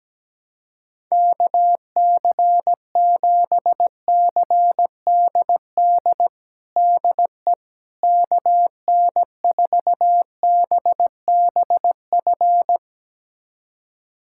Morse code, heard as a keyed tone, about 17 wpm, 700 Hz. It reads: KC7CDD DE KN4BBF
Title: USA Callsign DE Callsign - 6 Characters - 17wpm